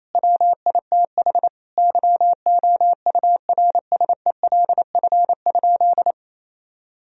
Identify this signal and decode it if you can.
Morse code, heard as a keyed tone, about 27 wpm, 700 Hz. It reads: WITH YOURSELF?
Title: Sentences from Top 1000 Words - Review Part 4 - 28wpm